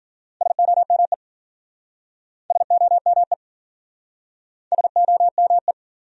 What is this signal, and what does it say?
Morse code, heard as a keyed tone, about 50 wpm, 700 Hz. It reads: SOME SOME SOME